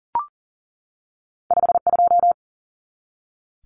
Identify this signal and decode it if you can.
Morse code, heard as a keyed tone, about 40 wpm, 700 Hz. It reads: 52
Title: Numbers - 2-Digit 40wpm